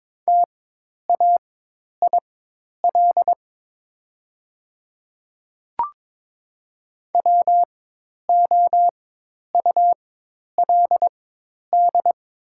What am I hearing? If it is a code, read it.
Morse code, heard as a keyed tone, about 22 wpm, spaced as for 10 wpm, 700 Hz. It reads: TAIL WOULD
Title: 4x Character Spacing - Top 500 Words - 22wpm